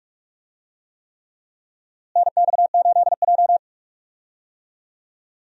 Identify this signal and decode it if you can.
Morse code, heard as a keyed tone, about 46 wpm, 700 Hz. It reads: NX8J
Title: USA Call Signs - 4 Characters - Repeated 3x - 45wpm